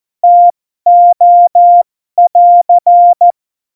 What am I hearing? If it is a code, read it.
Morse code, heard as a keyed tone, about 14 wpm, spaced as for 12 wpm, 700 Hz. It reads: TO <AR>
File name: Morsekurs / Witze